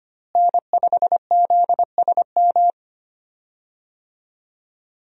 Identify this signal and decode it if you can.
Morse code, heard as a keyed tone, about 25 wpm, 700 Hz. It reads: N5ZSM
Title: USA Call Signs - 5 Characters - Repeated 3x - 25wpm